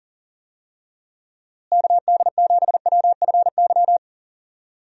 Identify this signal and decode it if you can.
Morse code, heard as a keyed tone, about 40 wpm, 700 Hz. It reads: KD7WFY
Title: USA Call Signs - 6 Characters - Repeated 2x - 40wpm